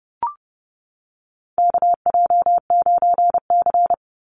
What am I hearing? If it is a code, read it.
Morse code, heard as a keyed tone, about 30 wpm, 700 Hz. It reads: KJ9C
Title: Call Signs - 30wpm